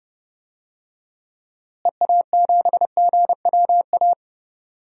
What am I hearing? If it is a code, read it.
Morse code, heard as a keyed tone, about 30 wpm, 700 Hz. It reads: EA7GWA
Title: DXCC-100 Call Signs - 6 Characters - Repeated 3x - 30wpm